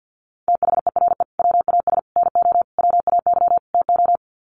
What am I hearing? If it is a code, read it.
Morse code, heard as a keyed tone, about 50 wpm, 700 Hz. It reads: THERE WAS NO WAY TO
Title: Sets of 5 Words Encore 50wpm